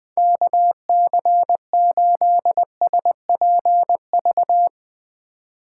Morse code, heard as a keyed tone, about 20 wpm, 700 Hz. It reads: KC8SPV
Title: USA Call Signs - 3x Character Spacing - 20wpm